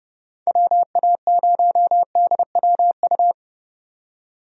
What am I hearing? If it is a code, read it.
Morse code, heard as a keyed tone, about 30 wpm, 700 Hz. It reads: WA0DWU
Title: USA Call Signs - 6 Characters - 30wpm